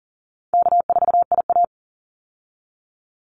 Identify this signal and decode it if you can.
Morse code, heard as a keyed tone, about 38 wpm, 700 Hz. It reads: K4IA
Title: Easy Call Signs Repeated 2x - 40wpm